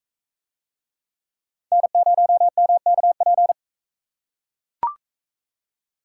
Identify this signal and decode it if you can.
Morse code, heard as a keyed tone, about 40 wpm, 700 Hz. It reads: N0MKP